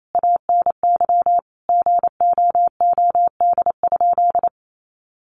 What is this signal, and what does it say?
Morse code, heard as a keyed tone, about 27 wpm, 700 Hz. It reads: ANY GOOD?
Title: Sentences from Top 1500 Words 28wpm